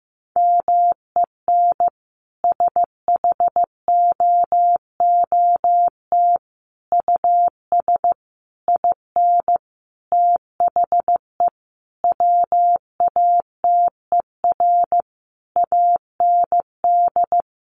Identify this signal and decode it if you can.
Morse code, heard as a keyed tone, about 15 wpm, 700 Hz. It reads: MEN SHOOT US IN THE WATER AND
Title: Sentences from Top 1200 Words 15wpm